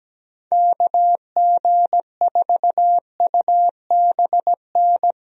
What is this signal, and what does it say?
Morse code, heard as a keyed tone, about 17 wpm, 700 Hz. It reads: KG4UBN
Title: USA Call Signs - 6 Characters - Repeated 3x - 17wpm